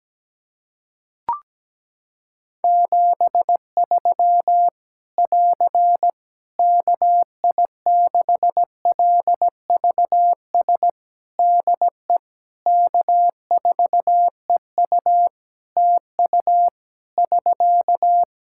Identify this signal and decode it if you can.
Morse code, heard as a keyed tone, about 17 wpm, 700 Hz. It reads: 73 <AR> KI6LVS DE K4EU TU <SK>
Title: K1USN SST QSO Practice 17wpm